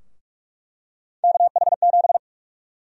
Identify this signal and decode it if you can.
Morse code, heard as a keyed tone, about 41 wpm, 700 Hz. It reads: KH7
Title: USA Call Sign Prefixes - 45wpm